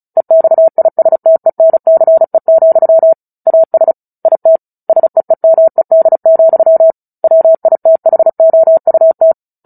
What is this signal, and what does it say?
Morse code, heard as a keyed tone, about 34 wpm, 650 Hz. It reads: EXISTENCE, AS IT SEEMED, WITHOUT
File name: Book Two / 09-Chapter 9 - 35 WPM